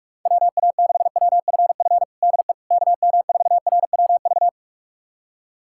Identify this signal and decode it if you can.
Morse code, heard as a keyed tone, about 46 wpm, 700 Hz. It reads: WA6WFF DE KM4RWV
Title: USA Callsign DE Callsign 45wpm